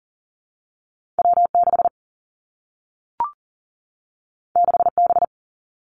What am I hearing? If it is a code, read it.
Morse code, heard as a keyed tone, about 42 wpm, 700 Hz. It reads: W 6 6 B
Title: Two Challenging Letter-Number Combinations 40wpm